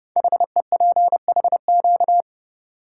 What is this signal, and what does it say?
Morse code, heard as a keyed tone, about 30 wpm, 700 Hz. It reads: HEPHQ
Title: Code Groups - 5 Letters - 30wpm